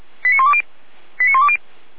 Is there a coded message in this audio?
Yes: five-tone selective call (ZVEI) from 0.2 to 1.6 s, heard as quick runs of five tones, tones of about 70 ms. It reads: 89120 89120